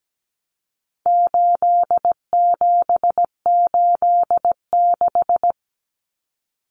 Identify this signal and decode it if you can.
Morse code, heard as a keyed tone, about 17 wpm, 700 Hz. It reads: 8786
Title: Numbers - 4-Digit 17wpm